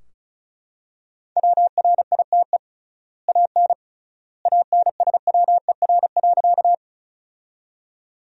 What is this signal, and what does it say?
Morse code, heard as a keyed tone, about 36 wpm, 700 Hz. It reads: WRITE AN ANSWER.
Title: 3x Word-Spacing - Sentences from Top 300 Words - 35wpm